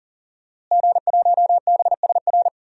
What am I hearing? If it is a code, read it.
Morse code, heard as a keyed tone, about 40 wpm, 700 Hz. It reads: G1BSR
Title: DXCC-100 Call Signs - 5 Characters - Repeated 3x - 40wpm